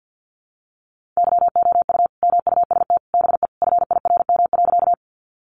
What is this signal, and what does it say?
Morse code, heard as a keyed tone, about 50 wpm, 700 Hz. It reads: YOU MUST BE FIRM.
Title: Sentences from Top 1400 Words 50wpm